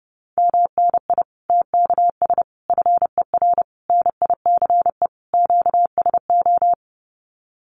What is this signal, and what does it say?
Morse code, heard as a keyed tone, about 30 wpm, 700 Hz. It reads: MNI TKS FER NICE QSO
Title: Common QSO Phrases 30wpm